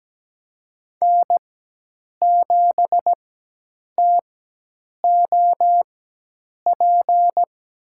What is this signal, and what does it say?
Morse code, heard as a keyed tone, about 17 wpm, spaced as for 8 wpm, 700 Hz. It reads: N7TOP
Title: USA Call Signs - 4x Character Spacing - 17wpm